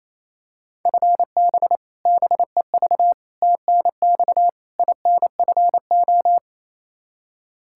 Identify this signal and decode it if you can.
Morse code, heard as a keyed tone, about 28 wpm, 700 Hz. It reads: FB BEV TNX INFO
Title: Practice QSO - Exchange 3 - More Info - 28wpm